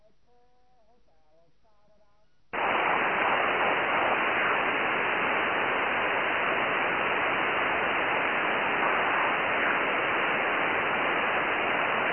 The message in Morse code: none